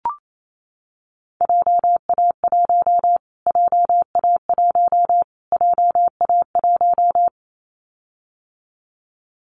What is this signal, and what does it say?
Morse code, heard as a keyed tone, about 28 wpm, 700 Hz. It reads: JA1 JA1 JA1
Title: Callsign Prefixes for Top 25 DXCC - Repeated 3x - 28wpm